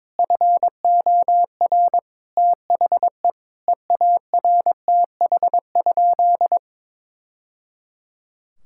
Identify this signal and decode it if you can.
Morse code, heard as a keyed tone, about 22 wpm, 700 Hz. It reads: FOR THE EARTH?
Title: Sentences from Top 1000 Words - Review Part 5 - 22wpm